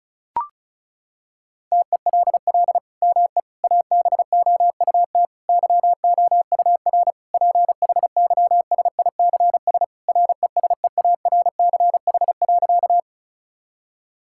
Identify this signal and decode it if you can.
Morse code, heard as a keyed tone, about 36 wpm, 700 Hz. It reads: TELL ME ABOUT YOUR PHYSICS RESEARCH.
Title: Sentences from Top 1000 Words - Review Part 5 - 35wpm